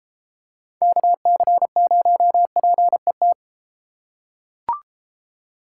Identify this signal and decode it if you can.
Morse code, heard as a keyed tone, about 31 wpm, 700 Hz. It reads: KC0PET